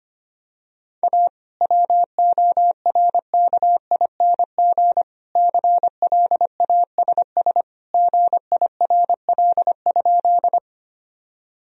Morse code, heard as a keyed tone, about 25 wpm, 700 Hz. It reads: A WORKING CLASS GIRL?
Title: Sentences from Top 1000 Words - Review Part 4 - 25wpm